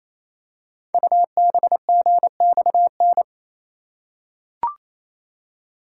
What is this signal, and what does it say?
Morse code, heard as a keyed tone, about 27 wpm, 700 Hz. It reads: UBGXN
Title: Code Groups - 5 Letters - 28wpm